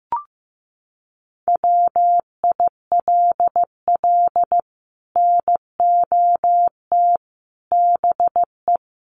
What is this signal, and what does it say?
Morse code, heard as a keyed tone, about 15 wpm, 700 Hz. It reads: WILL NOT BE
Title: Sets of 5 Words 15wpm